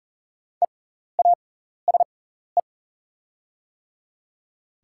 Morse code, heard as a keyed tone, about 40 wpm, spaced as for 14 wpm, 700 Hz. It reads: EASE
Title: 6x Character Spacing - Top 300 Words - 40wpm